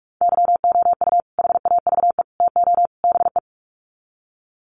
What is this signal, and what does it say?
Morse code, heard as a keyed tone, about 44 wpm, 700 Hz. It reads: YOU HAVE TO BE
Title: Sets of 4 Words 45wpm